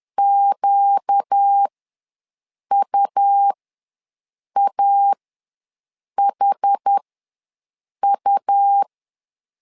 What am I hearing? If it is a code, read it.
Morse code, heard as a keyed tone, about 10 wpm, spaced as for 6 wpm, 800 Hz. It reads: QUAHU
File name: LEZIONI / 4 Q ws 10wpm 2306